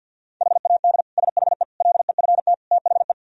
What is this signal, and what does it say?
Morse code, heard as a keyed tone, about 50 wpm, 700 Hz. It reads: HAD SHE LEFT THE
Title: Sentences from Top 1000 Words - Review Part 3 - 50wpm